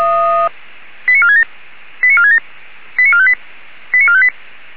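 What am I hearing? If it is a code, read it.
Five-tone selective call (ZVEI), heard as quick runs of five tones, tones of about 70 ms. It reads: 89467 89467 89458 89458
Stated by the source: ZVEI five-tone call 89355 received by SDR, recorded by feuerwehr-orlamuende fire brigade.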